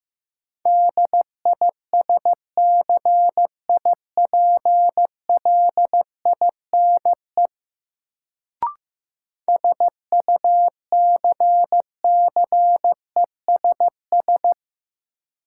Morse code, heard as a keyed tone, about 15 wpm, 700 Hz. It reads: DISCIPLINE SUCCESS
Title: Sentences from Top 900 Words - New Word Drill - 15wpm